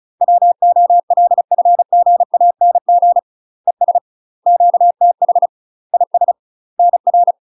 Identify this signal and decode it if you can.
Morse code, heard as a keyed tone, about 34 wpm, 700 Hz. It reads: WOLFGANG ES QTH IS NR
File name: UK MASTER QSOs TRAINING / 35WPM / UK MASTER QSOs 0024